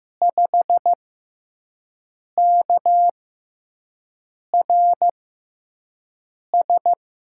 Morse code, heard as a keyed tone, about 15 wpm, spaced as for 5 wpm, 700 Hz. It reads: 5KRS
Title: USA Call Signs - 5 Characters - 6x Character Spacing - 15wpm